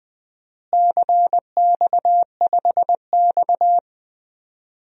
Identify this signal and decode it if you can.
Morse code, heard as a keyed tone, about 20 wpm, 700 Hz. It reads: CX5X
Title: DXCC-100 Call Signs - 4 Characters - Repeated 2x - 20wpm